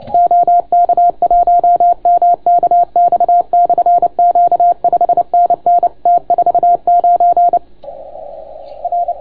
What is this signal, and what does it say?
Morse code, heard as a keyed tone, about 30 wpm, 690 Hz. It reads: OK1MKX/Q 5NN T49 R